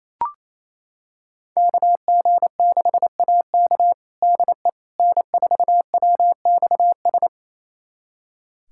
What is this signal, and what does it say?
Morse code, heard as a keyed tone, about 27 wpm, 700 Hz. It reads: KG6AK DE N4WXS
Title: USA Callsign DE Callsign - 5 Characters - 28wpm